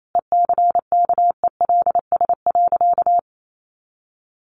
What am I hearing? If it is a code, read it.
Morse code, heard as a keyed tone, about 28 wpm, 700 Hz. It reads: ECKELS.
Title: A Sound Of Thunder 28wpm